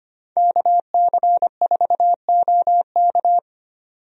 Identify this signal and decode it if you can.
Morse code, heard as a keyed tone, about 25 wpm, 700 Hz. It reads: KC4OK
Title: USA Call Signs - 5 Characters - 25wpm